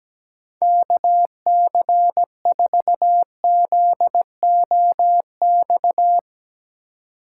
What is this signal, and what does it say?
Morse code, heard as a keyed tone, about 17 wpm, 700 Hz. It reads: KC4ZOX
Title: USA Call Signs - 6 Characters - 3x Character Spacing - 17wpm